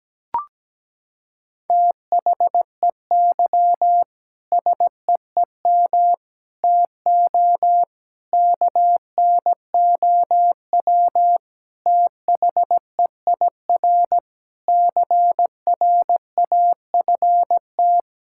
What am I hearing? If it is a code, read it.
Morse code, heard as a keyed tone, about 17 wpm, 700 Hz. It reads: THEY SEEM TO KNOW THEIR CRAFT
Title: Sentences from Top 1000 Words - Review Part 6 - 17wpm